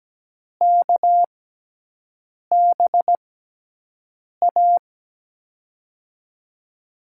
Morse code, heard as a keyed tone, about 17 wpm, spaced as for 6 wpm, 700 Hz. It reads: KBA
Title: USA Call Signs - 6 Characters - 6x Character Spacing - 17wpm